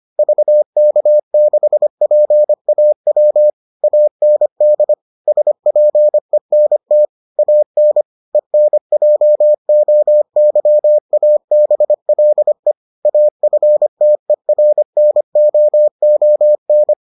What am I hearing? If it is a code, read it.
Morse code, heard as a keyed tone, about 25 wpm, 600 Hz. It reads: VK6PAW AND SPENT AN ENJOYABLE AFTERNOON